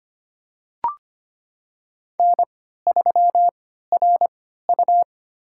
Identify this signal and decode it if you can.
Morse code, heard as a keyed tone, about 25 wpm, spaced as for 14 wpm, 700 Hz. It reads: N3RU
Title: USA Call Signs - 4 Characters - 3x Character Spacing - 25wpm